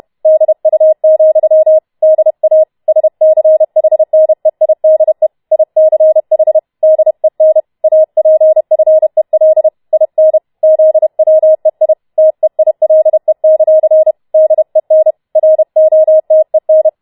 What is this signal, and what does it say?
Morse code, heard as a keyed tone, about 31 wpm, 630 Hz. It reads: DU, DA SCHNEIDE ICH DEN APFEL IN ZWEI TEILE; DEN ROTEN